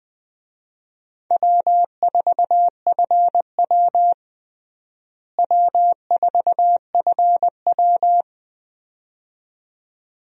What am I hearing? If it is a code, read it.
Morse code, heard as a keyed tone, about 20 wpm, 700 Hz. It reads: W4FW W4FW
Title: USA Call Signs - 4 Characters - Repeated 2x - 20wpm